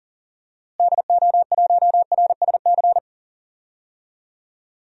Morse code, heard as a keyed tone, about 40 wpm, 700 Hz. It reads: DO1RSC
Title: DXCC-100 Call Signs - 6 Characters - Repeated 2x - 40wpm